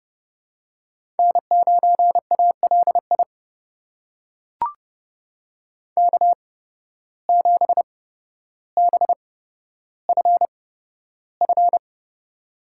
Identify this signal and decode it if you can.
Morse code, heard as a keyed tone, about 30 wpm, spaced as for 13 wpm, 700 Hz. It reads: N9ALI K7BFF
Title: USA Call Signs - 5 Characters - 8x Character Spacing - 30wpm